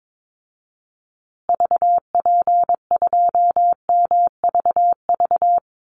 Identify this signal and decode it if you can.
Morse code, heard as a keyed tone, about 22 wpm, 700 Hz. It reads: VP2MVV
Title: Call Signs Repeated 2x - 22wpm